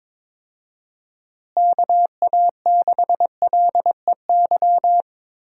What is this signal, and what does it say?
Morse code, heard as a keyed tone, about 22 wpm, 700 Hz. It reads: KA6LEY
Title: USA Call Signs - 6 Characters - Repeated 2x - 22wpm